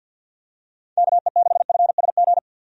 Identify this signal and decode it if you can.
Morse code, heard as a keyed tone, about 51 wpm, 700 Hz. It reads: KE6FSG
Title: USA Call Signs - 6 Characters - 50wpm